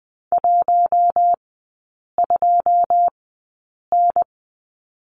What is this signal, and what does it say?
Morse code, heard as a keyed tone, about 20 wpm, spaced as for 8 wpm, 700 Hz. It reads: 12N
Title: 107 Lesson 18e - Letter m - Three Characters 20wpm